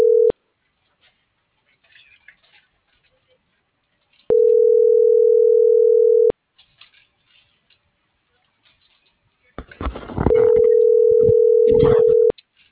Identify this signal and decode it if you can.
Ringback tone (North America), 440 + 480 Hz.